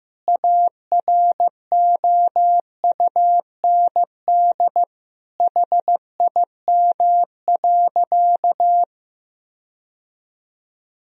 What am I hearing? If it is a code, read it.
Morse code, heard as a keyed tone, about 16 wpm, 700 Hz. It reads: AROUND HIM.
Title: Sentences from Top 1000 Words - Review Part 3 - 15wpm